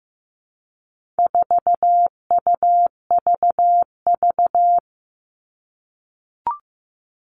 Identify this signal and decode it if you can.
Morse code, heard as a keyed tone, about 15 wpm, 700 Hz. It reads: 4UVV
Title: All Cut Numbers - 4-Digit 15wpm